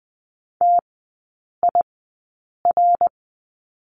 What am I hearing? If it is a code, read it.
Morse code, heard as a keyed tone, about 20 wpm, spaced as for 8 wpm, 700 Hz. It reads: TIR
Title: 093 Lesson 16e - Letter c - Three Characters 20wpm